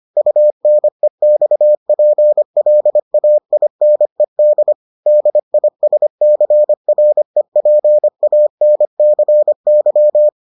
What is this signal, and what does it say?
Morse code, heard as a keyed tone, about 25 wpm, 600 Hz. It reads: UNEXPLAINED DISCREPANCY